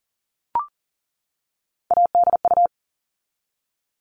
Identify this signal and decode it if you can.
Morse code, heard as a keyed tone, about 39 wpm, 700 Hz. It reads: ADU
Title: All Cut Numbers - 3-Digit 40wpm